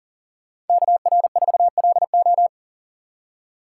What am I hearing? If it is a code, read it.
Morse code, heard as a keyed tone, about 52 wpm, 700 Hz. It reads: KR4LO